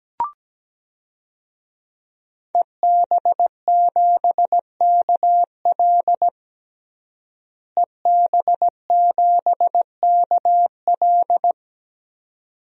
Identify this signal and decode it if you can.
Morse code, heard as a keyed tone, about 17 wpm, 700 Hz. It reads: EB7KL EB7KL
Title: DXCC-100 Call Signs - 5 Characters - Repeated 2x - 17wpm